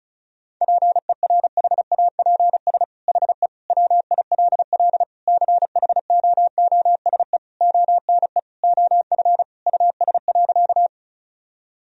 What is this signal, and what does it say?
Morse code, heard as a keyed tone, about 36 wpm, 700 Hz. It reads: PERHAPS HE WILL CHOOSE ONE OF US.
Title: Sentences from Top 1000 Words - Review Part 4 - 35wpm